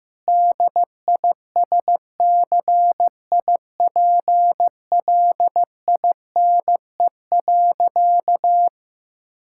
Morse code, heard as a keyed tone, about 15 wpm, 700 Hz. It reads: DISCIPLINE.
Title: Sentences from Top 1000 Words - Review Part 6 - 15wpm